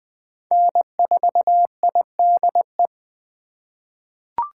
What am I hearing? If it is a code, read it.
Morse code, heard as a keyed tone, about 20 wpm, 700 Hz. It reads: N4IDE
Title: USA Call Signs - 5 Characters - 3x Character Spacing - 20wpm